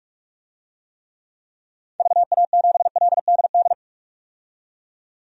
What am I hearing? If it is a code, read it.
Morse code, heard as a keyed tone, about 46 wpm, 700 Hz. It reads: VA7LDD
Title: DXCC-100 Call Signs - 6 Characters - 45wpm